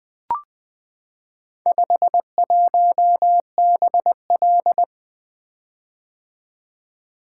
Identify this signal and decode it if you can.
Morse code, heard as a keyed tone, about 20 wpm, 700 Hz. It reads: 51BL
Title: Four Letter-Number Combinations - 20wpm